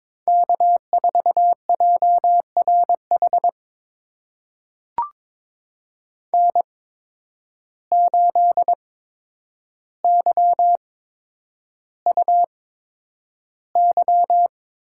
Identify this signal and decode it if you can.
Morse code, heard as a keyed tone, about 22 wpm, spaced as for 9 wpm, 700 Hz. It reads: K4JRH N8YUY
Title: USA Call Signs - 5 Characters - 8x Character Spacing - 22wpm